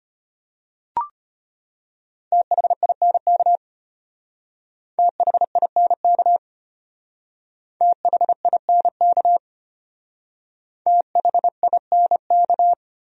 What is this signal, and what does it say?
Morse code, heard as a keyed tone, about 31 wpm, 700 Hz. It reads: THINK THINK THINK THINK
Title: Speed-Racing Top 200 Words 25wpm